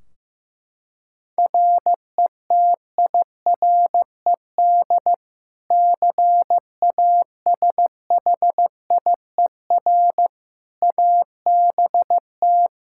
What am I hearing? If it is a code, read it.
Morse code, heard as a keyed tone, about 15 wpm, 700 Hz. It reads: RETIRED CASHIER ABT
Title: K1USN SST QSO Practice 15wpm